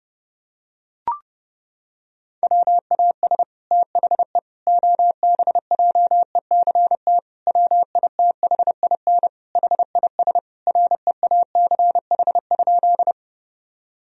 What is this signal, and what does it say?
Morse code, heard as a keyed tone, about 30 wpm, 700 Hz. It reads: WAS THE OBJECT WITHIN HIS REACH?
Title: Sentences from Top 1000 Words - Review Part 4 - 30wpm